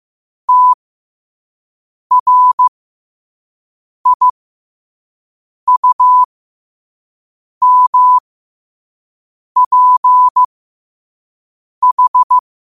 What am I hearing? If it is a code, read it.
Morse code, heard as a keyed tone, about 15 wpm, spaced as for 5 wpm, 1000 Hz. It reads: TRIUMPH
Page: Quotes for Fri, 15 Aug 2025 in Morse Code at 5 words per minute.